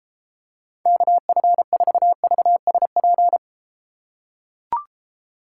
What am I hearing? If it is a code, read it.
Morse code, heard as a keyed tone, about 33 wpm, 700 Hz. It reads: KF4VSP